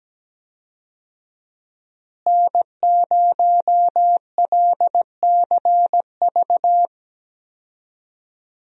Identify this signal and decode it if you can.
Morse code, heard as a keyed tone, about 17 wpm, 700 Hz. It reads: N0LCV
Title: USA Call Signs - 5 Characters - Repeated 3x - 17wpm